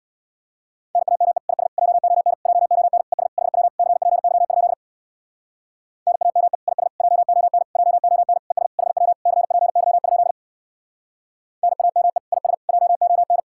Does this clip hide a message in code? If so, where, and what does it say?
Yes: Morse code from 0.9 to 13.4 s, heard as a keyed tone, about 67 wpm, 700 Hz. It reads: NAME IS JON JON ES HW CPY? NAME IS JON JON ES HW CPY? NAME IS JON